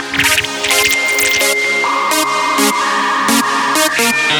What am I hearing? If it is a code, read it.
Dial-up modem handshake.